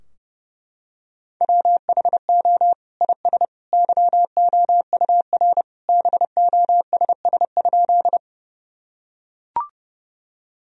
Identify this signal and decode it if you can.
Morse code, heard as a keyed tone, about 30 wpm, 700 Hz. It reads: WHO IS YOUR BOSS?
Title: Sentences from Top 1000 Words - Review Part 4 - 30wpm